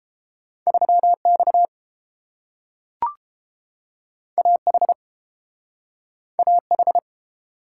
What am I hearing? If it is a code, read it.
Morse code, heard as a keyed tone, about 32 wpm, 700 Hz. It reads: 3 X A H A H